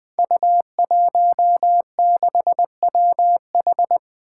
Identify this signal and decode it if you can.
Morse code, heard as a keyed tone, about 20 wpm, 700 Hz. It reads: U16WH
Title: Five Challenging Letter-Number Combinations 20wpm